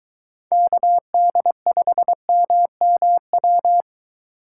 Morse code, heard as a keyed tone, about 23 wpm, 700 Hz. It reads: KD5MMW